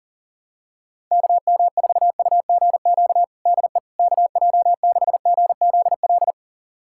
Morse code, heard as a keyed tone, about 40 wpm, 700 Hz. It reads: KM4UGQ DE KJ6GZL
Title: USA Callsign DE Callsign 40wpm